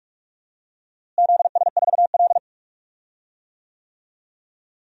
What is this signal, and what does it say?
Morse code, heard as a keyed tone, about 46 wpm, 700 Hz. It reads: ZS4L
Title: DXCC-100 Call Signs - 4 Characters - Repeated 2x - 45wpm